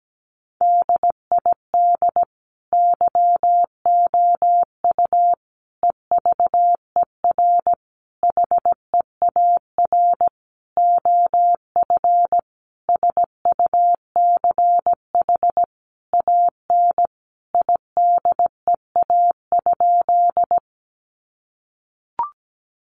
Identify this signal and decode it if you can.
Morse code, heard as a keyed tone, about 17 wpm, 700 Hz. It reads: DID YOU EVER HEAR OF SUCH AN IDEA?
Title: Sentences from Top 400 Words 17wpm